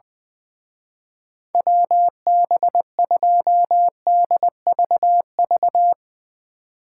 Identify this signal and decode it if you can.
Morse code, heard as a keyed tone, about 22 wpm, 700 Hz. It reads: WB2DVV
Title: Speed Racing - USA 2x3 Call Signs 20wpm